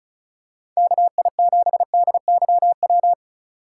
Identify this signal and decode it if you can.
Morse code, heard as a keyed tone, about 36 wpm, 690 Hz. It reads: KI7DYW
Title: USA Call Signs - 6 Characters - 3x Character Spacing - 35wpm